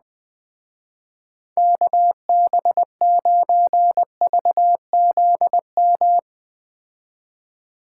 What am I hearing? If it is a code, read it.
Morse code, heard as a keyed tone, about 20 wpm, 700 Hz. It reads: KB9VZM